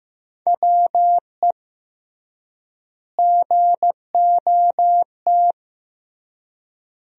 Morse code, heard as a keyed tone, about 15 wpm, 700 Hz. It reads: WE GOT
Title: 3x Word-Spacing - Sentences from Top 300 Words - 15wpm